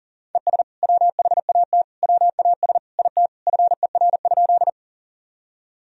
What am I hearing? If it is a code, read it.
Morse code, heard as a keyed tone, about 40 wpm, 700 Hz. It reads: ES WHAT WAS IT FER?
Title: Sentences from Top 100 Words and Abbreviations 40wpm